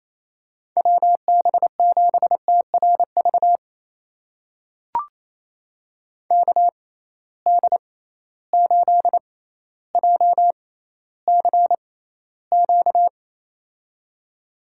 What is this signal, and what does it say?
Morse code, heard as a keyed tone, about 28 wpm, spaced as for 14 wpm, 700 Hz. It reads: WB7TRV KD8JCQ
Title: USA Call Signs - 6 Characters - 6x Character Spacing - 28wpm